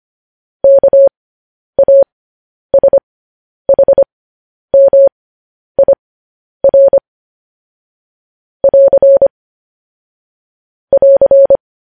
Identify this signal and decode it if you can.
Morse code, heard as a keyed tone, about 25 wpm, spaced as for 10 wpm, 550 Hz. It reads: KASHMIR <AR> <AR>
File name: podcast / News headlines-1x-25x10